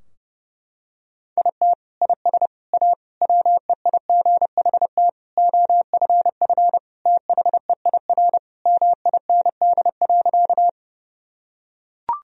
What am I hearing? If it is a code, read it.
Morse code, heard as a keyed tone, about 32 wpm, 700 Hz. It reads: IT IS A WEIGHT OFF THEIR MIND.
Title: Sentences from Top 1000 Words - Review Part 3 - 30wpm